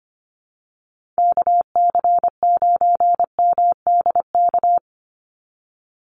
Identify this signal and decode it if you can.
Morse code, heard as a keyed tone, about 25 wpm, 700 Hz. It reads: KC9MDK
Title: USA Call Signs 25wpm